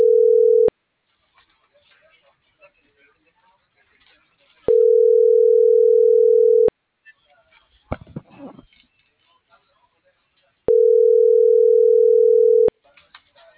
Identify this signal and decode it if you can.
Ringback tone (North America), 440 + 480 Hz.